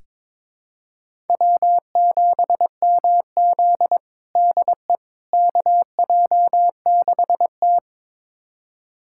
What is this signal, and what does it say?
Morse code, heard as a keyed tone, about 22 wpm, 700 Hz. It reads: W7MZ DE KJ6T
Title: USA Callsign DE Callsign - 4 Characters - 22wpm